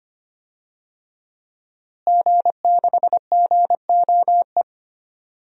Morse code, heard as a keyed tone, about 25 wpm, 700 Hz. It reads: G6GOE
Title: DXCC-100 Call Signs - 5 Characters - Repeated 2x - 25wpm